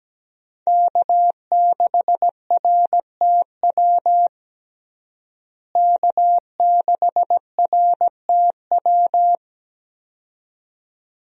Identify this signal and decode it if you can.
Morse code, heard as a keyed tone, about 17 wpm, 700 Hz. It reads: K6RTW K6RTW
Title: USA Call Signs - 5 Characters - Repeated 2x - 17wpm